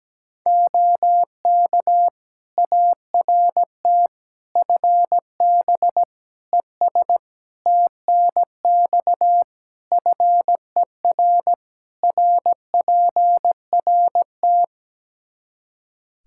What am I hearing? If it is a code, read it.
Morse code, heard as a keyed tone, about 17 wpm, 700 Hz. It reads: OK ART FB ES TNX FER RPRT
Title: Practice QSO - Exchange 2 - Info - 17wpm